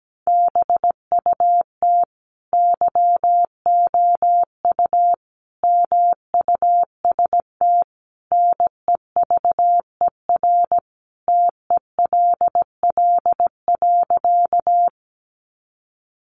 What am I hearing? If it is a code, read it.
Morse code, heard as a keyed tone, about 17 wpm, 700 Hz. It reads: BUT YOU MUST NEVER TELL.
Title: Sentences from Top 300 Words 17wpm